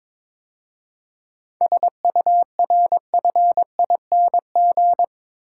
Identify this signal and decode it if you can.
Morse code, heard as a keyed tone, about 26 wpm, 700 Hz. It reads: SURFING
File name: Easy Words - Speed-Racing with Hints - 22wpm